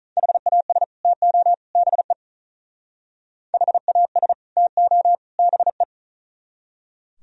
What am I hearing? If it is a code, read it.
Morse code, heard as a keyed tone, about 46 wpm, 700 Hz. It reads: HAS TO BE HAS TO BE